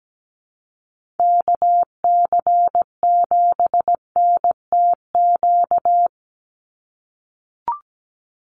Morse code, heard as a keyed tone, about 17 wpm, 700 Hz. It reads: KC7NTQ
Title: USA Call Signs 17wpm